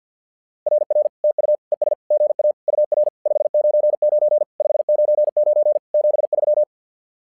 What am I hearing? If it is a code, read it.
Morse code, heard as a keyed tone, about 50 wpm, 600 Hz. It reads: RR TU ES GA UR 599 599 73
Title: SOTA QSO Practice 50wpm